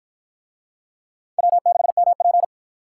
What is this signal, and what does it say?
Morse code, heard as a keyed tone, about 53 wpm, 700 Hz. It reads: W6MP